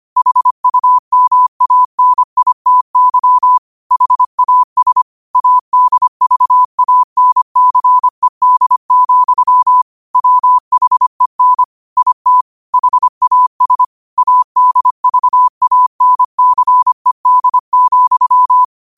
Morse code, heard as a keyed tone, about 25 wpm, 1000 Hz. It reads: HUMANITY HAS ADVANCED, WHEN IT HAS ADVANCED,